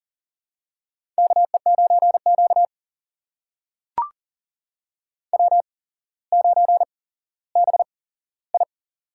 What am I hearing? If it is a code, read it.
Morse code, heard as a keyed tone, about 42 wpm, spaced as for 17 wpm, 700 Hz. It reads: KE9Q W9BI
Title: USA Call Signs - 4 Characters - 8x Character Spacing - 40wpm